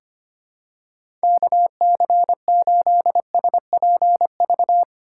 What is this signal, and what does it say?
Morse code, heard as a keyed tone, about 25 wpm, 700 Hz. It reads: KC8SPV
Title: USA Call Signs - 6 Characters - 4x Character Spacing - 25wpm